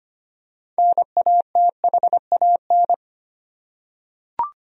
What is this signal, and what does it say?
Morse code, heard as a keyed tone, about 25 wpm, 700 Hz. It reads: NATHAN
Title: USA Common Names - 25wpm